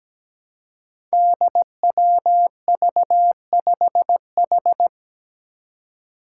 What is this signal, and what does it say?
Morse code, heard as a keyed tone, about 17 wpm, 700 Hz. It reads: DWV5H
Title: Five Challenging Letter-Number Combinations 17wpm